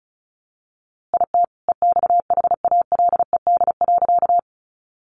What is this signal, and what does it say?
Morse code, heard as a keyed tone, about 34 wpm, 700 Hz. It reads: IT EXHALED.
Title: A Sound Of Thunder 35wpm